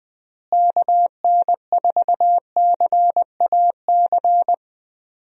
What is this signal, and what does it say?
Morse code, heard as a keyed tone, about 20 wpm, 700 Hz. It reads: KN4CAC
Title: USA Call Signs - 6 Characters - Repeated 2x - 20wpm